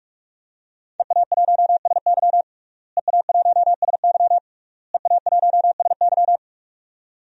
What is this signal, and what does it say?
Morse code, heard as a keyed tone, about 41 wpm, 700 Hz. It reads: EA1SY EA1SY EA1SY
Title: DXCC-100 Call Signs - 5 Characters - Repeated 3x - 45wpm